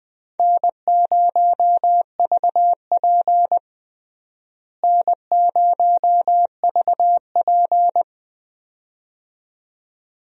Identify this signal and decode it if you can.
Morse code, heard as a keyed tone, about 20 wpm, 700 Hz. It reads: N0VP N0VP
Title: USA Call Signs - 4 Characters - Repeated 2x - 20wpm